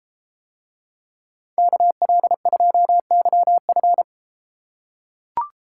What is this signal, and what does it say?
Morse code, heard as a keyed tone, about 30 wpm, 700 Hz. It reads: KL2YF